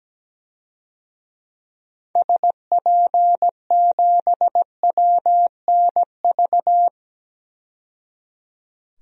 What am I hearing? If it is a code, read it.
Morse code, heard as a keyed tone, about 17 wpm, 700 Hz. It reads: SP7WNV
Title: DXCC-100 Call Signs - 6 Characters - 17wpm